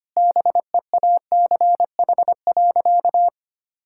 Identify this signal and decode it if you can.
Morse code, heard as a keyed tone, about 25 wpm, 700 Hz. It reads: BEACH.
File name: practice / 25wpm / Difficult Sentences with Hints - 25wpm